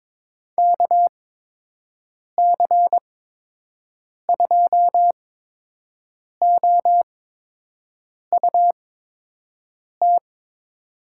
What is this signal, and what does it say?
Morse code, heard as a keyed tone, about 22 wpm, spaced as for 6 wpm, 700 Hz. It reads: KC2OUT
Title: USA Call Signs - 8x Character Spacing - 22wpm